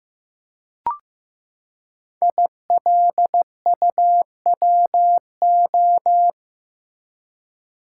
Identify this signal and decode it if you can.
Morse code, heard as a keyed tone, about 16 wpm, 700 Hz. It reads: ILUWO
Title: Code Groups - 5 Letters - 15wpm